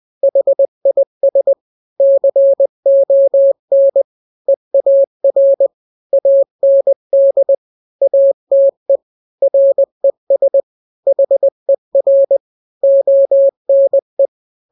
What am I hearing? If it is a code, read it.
Morse code, heard as a keyed tone, about 20 wpm, 550 Hz. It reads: HIS CON EAR AND ATE RES HER ONE
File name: n-grams / 3-grams 20 18 WPM